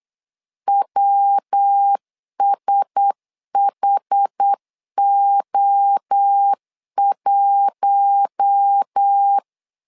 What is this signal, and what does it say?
Morse code, heard as a keyed tone, about 8 wpm, 800 Hz. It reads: WSHO1
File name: LEZIONI / 4 Q1S 8wpm 2710